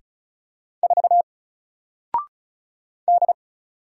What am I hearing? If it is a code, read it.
Morse code, heard as a keyed tone, about 36 wpm, 700 Hz. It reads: 4 D
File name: Single-Letter-Number-35wpm